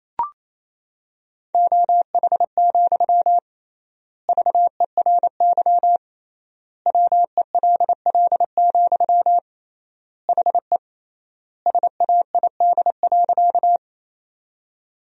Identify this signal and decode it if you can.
Morse code, heard as a keyed tone, about 28 wpm, 700 Hz. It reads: OH, VERY WELL, HE SAID.
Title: 3x Word-Spacing - Sentences from Top 500 Words - 28wpm